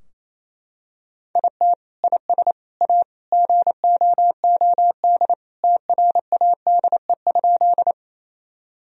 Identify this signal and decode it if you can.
Morse code, heard as a keyed tone, about 28 wpm, 700 Hz. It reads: IT IS A GOOD TRADE?
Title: Sentences from Top 1000 Words - Review Part 4 - 28wpm